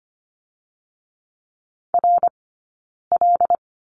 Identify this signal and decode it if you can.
Morse code, heard as a keyed tone, about 25 wpm, 700 Hz. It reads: R L
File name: course / 25:10wpm / 099 Lesson 17d - Letter u - Two Characters 25wpm